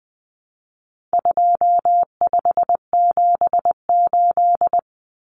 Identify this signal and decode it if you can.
Morse code, heard as a keyed tone, about 20 wpm, 700 Hz. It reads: 2578
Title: Common Cut Numbers - 4-Digit 20wpm